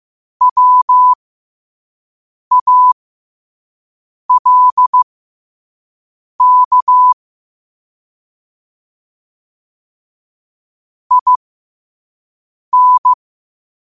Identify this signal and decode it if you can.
Morse code, heard as a keyed tone, about 15 wpm, spaced as for 5 wpm, 1000 Hz. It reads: WALK IN
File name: qotdfiles / 5wpm